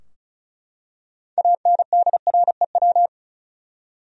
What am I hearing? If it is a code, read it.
Morse code, heard as a keyed tone, about 34 wpm, 700 Hz. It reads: ANDREW
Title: USA Common Names - 35wpm